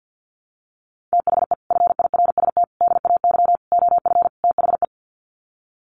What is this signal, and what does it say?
Morse code, heard as a keyed tone, about 50 wpm, 700 Hz. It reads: THE FIRST DAY OF THE
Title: Sets of 5 Words Encore2 50wpm